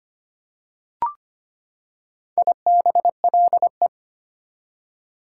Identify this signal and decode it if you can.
Morse code, heard as a keyed tone, about 25 wpm, 700 Hz. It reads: IBLE
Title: Prefixes and Suffixes 25wpm